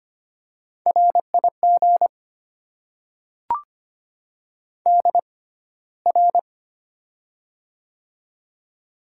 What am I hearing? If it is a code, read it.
Morse code, heard as a keyed tone, about 25 wpm, spaced as for 15 wpm, 700 Hz. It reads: RIG DR
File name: Top 90 QSO Elements - 6x Character Spacing - 25wpm